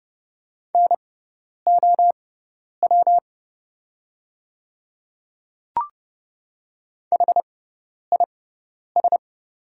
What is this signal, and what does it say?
Morse code, heard as a keyed tone, about 30 wpm, spaced as for 10 wpm, 700 Hz. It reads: NOW HIS
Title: 6x Character Spacing - Top 100 Words - 30wpm